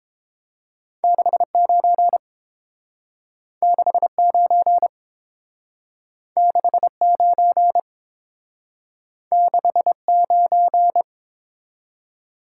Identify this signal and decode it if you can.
Morse code, heard as a keyed tone, about 30 wpm, 700 Hz. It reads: 69 69 69 69